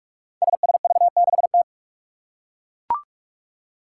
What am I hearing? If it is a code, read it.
Morse code, heard as a keyed tone, about 42 wpm, 700 Hz. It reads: SSV6T
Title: Five Challenging Letter-Number Combinations 45wpm